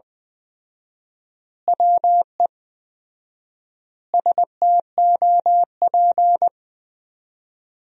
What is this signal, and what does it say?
Morse code, heard as a keyed tone, about 19 wpm, 700 Hz. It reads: WE STOP
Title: 4x Word-Spacing - Sentences from Top 300 Words - 20wpm